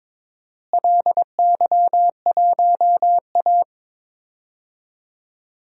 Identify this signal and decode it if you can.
Morse code, heard as a keyed tone, about 22 wpm, 700 Hz. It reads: LY1A
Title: DXCC-100 Call Signs - 4 Characters - Repeated 3x - 22wpm